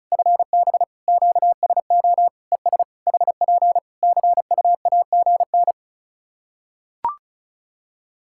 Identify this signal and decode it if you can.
Morse code, heard as a keyed tone, about 36 wpm, 700 Hz. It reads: FB QSO ES HP CUAGN
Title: Practice QSO - Exchange 4 - Ending - 35wpm